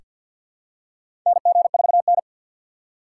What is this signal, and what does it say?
Morse code, heard as a keyed tone, about 47 wpm, 700 Hz. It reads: NG4N
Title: USA Call Signs - 3x Character Spacing - 50wpm